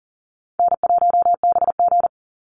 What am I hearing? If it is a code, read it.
Morse code, heard as a keyed tone, about 40 wpm, 700 Hz. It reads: N1BG
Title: Easy Call Signs Repeated 3x - 40wpm